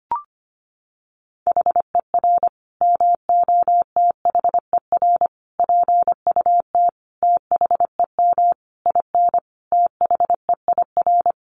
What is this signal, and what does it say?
Morse code, heard as a keyed tone, about 25 wpm, 700 Hz. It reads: HER MOTHER PUT THEM IN THEIR
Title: Sentences from Top 800 Words 25wpm